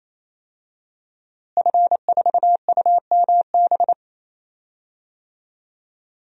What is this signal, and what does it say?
Morse code, heard as a keyed tone, about 28 wpm, 700 Hz. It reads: F4UMB
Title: DXCC-100 Call Signs - 5 Characters - Repeated 3x - 28wpm